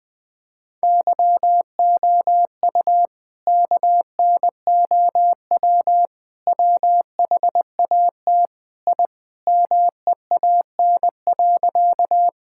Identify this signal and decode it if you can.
Morse code, heard as a keyed tone, about 20 wpm, 700 Hz. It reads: YOU KNOW WHAT I MEAN.
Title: Sentences with 4-Word Phrases - Spoken Beginning - 20wpm